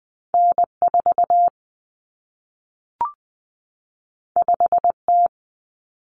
Morse code, heard as a keyed tone, about 20 wpm, 700 Hz. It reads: N4 5T
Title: Common Cut Numbers - 2-Digit 20wpm